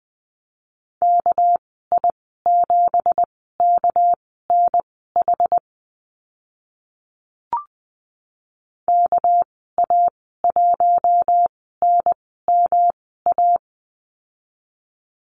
Morse code, heard as a keyed tone, about 20 wpm, spaced as for 14 wpm, 700 Hz. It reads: KI7KNH KA1NMA
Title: 238 Lesson 38g - Letter q - Call Signs 20wpm